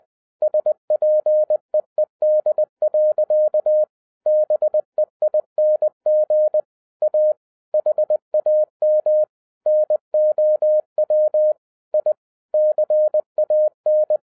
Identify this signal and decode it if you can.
Morse code, heard as a keyed tone, about 20 wpm, 610 Hz. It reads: SPEED. BEING A HAM NOW I CAN
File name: 2011-07-mp3 / n08 A LOT HAS CHANGED 20wpm